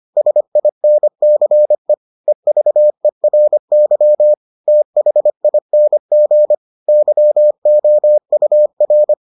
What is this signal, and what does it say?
Morse code, heard as a keyed tone, about 25 wpm, 600 Hz. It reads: SINCE EVERYTHING YOUR